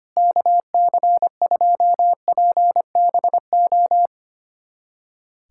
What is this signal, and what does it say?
Morse code, heard as a keyed tone, about 25 wpm, 700 Hz. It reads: KC2PBO
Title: USA Callsign DE Callsign 25wpm